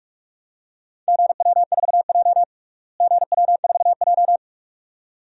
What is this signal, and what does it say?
Morse code, heard as a keyed tone, about 44 wpm, 700 Hz. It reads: GW4J GW4J
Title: DXCC-100 Call Signs - 4 Characters - Repeated 2x - 45wpm